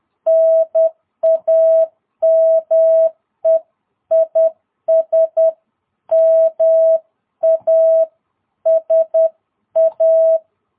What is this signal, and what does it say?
Morse code, heard as a keyed tone, about 10 wpm, 660 Hz. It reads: NAME IS MASA
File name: jh8cbh / 2022cw